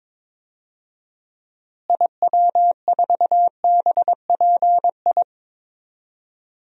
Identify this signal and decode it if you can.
Morse code, heard as a keyed tone, about 22 wpm, 700 Hz. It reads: IW4BPI
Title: DXCC-100 Call Signs - 6 Characters - Repeated 2x - 22wpm